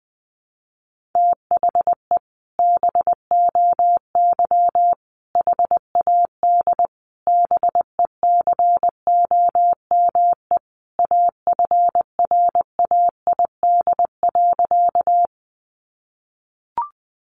Morse code, heard as a keyed tone, about 20 wpm, 700 Hz. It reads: THE BOY HAD BECOME AFRAID.
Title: Sentences from Top 1500 Words 20wpm